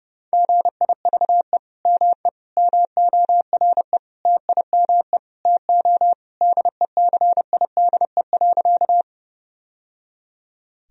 Morse code, heard as a keyed tone, about 30 wpm, 700 Hz. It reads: GIVE ME MORE TIME TO DECIDE.
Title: Sentences from Top 1000 Words - Review Part 3 - 30wpm